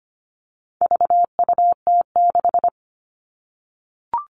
Morse code, heard as a keyed tone, about 25 wpm, 700 Hz. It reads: VUT6
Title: All Cut Numbers - 4-Digit 25wpm